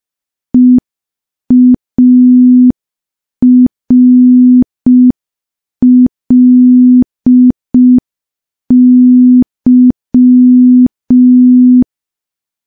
Morse code, heard as a keyed tone, about 5 wpm, 250 Hz. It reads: EARLY
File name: morse / morse-training-05WPM